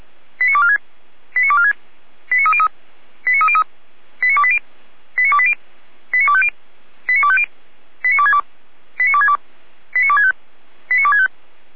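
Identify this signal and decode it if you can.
Five-tone selective call (ZVEI), heard as quick runs of five tones, tones of about 70 ms. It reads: 89246 89246 89303 89303 89280 89280 89250 89250 89262 89262 89265 89265
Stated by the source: ZVEI five-tone selective call 89235, SDR recording.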